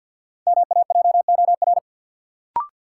Morse code, heard as a keyed tone, about 51 wpm, 700 Hz. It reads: MAJOR
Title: Sentences from Top 600 Words - New Word Drill - 50wpm